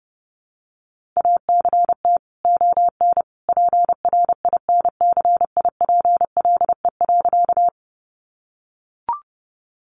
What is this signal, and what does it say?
Morse code, heard as a keyed tone, about 30 wpm, 700 Hz. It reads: ACT ON PRINCIPLE.
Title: Sentences from Top 1400 Words 30wpm